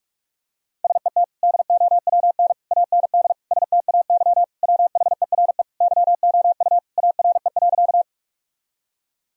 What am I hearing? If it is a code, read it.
Morse code, heard as a keyed tone, about 41 wpm, 700 Hz. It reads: SET DOWN AND STAY WHERE YOU ARE.
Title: Sentences from Top 1000 Words - Review Part 1 - 45wpm